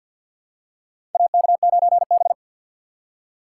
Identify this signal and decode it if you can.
Morse code, heard as a keyed tone, about 46 wpm, 700 Hz. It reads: AK9B
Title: USA Call Signs - 4 Characters - 6x Character Spacing - 50wpm